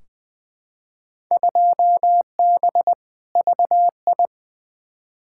Morse code, heard as a keyed tone, about 20 wpm, 700 Hz. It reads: 2B VI
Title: Field Day Exchanges Encore - 20wpm